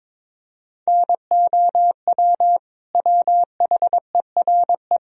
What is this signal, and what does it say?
Morse code, heard as a keyed tone, about 22 wpm, 700 Hz. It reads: NOW WHERE
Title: Sentences from Top 200 Words and Abbreviations 22wpm